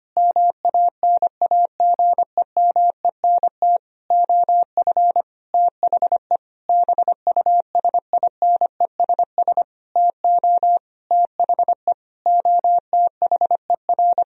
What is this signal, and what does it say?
Morse code, heard as a keyed tone, about 25 wpm, 700 Hz. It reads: MANAGEMENT OF THE BUSINESS TO THE OTHER
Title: Sentences from Top 1000 Words - Review Part 4 - 25wpm